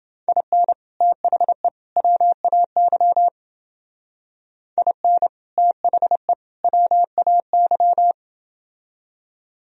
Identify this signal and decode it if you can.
Morse code, heard as a keyed tone, about 29 wpm, 700 Hz. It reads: IN THE WAY IN THE WAY